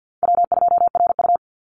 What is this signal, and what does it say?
Morse code, heard as a keyed tone, about 50 wpm, 700 Hz. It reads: W2RU
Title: Easy Call Signs - 50wpm